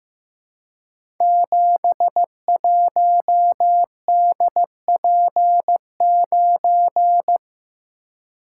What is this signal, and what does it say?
Morse code, heard as a keyed tone, about 15 wpm, 700 Hz. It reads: 71DP9
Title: Code Groups - 5 Letter:Numbers - 15wpm